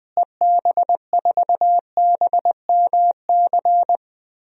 Morse code, heard as a keyed tone, about 20 wpm, 700 Hz. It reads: EB4BMC
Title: DXCC-100 Call Signs - 6 Characters - Repeated 3x - 20wpm